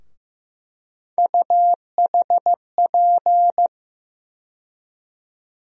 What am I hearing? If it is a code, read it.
Morse code, heard as a keyed tone, about 15 wpm, 700 Hz. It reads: UHP
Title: USA Call Sign Postfixes - 3 Letters 15wpm